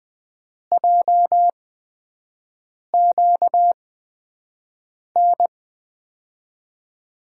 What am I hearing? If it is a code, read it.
Morse code, heard as a keyed tone, about 20 wpm, spaced as for 5 wpm, 700 Hz. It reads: JQN
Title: USA Call Signs - 8x Character Spacing - 20wpm